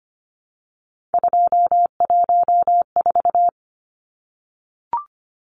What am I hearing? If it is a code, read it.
Morse code, heard as a keyed tone, about 25 wpm, 700 Hz. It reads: 214
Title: Numbers - 3-Digit 25wpm